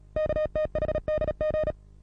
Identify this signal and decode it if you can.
Morse code, heard as a keyed tone, about 37 wpm, 620 Hz. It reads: KTHDG
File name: repos / Morsekurs / DJ4UF Morsen27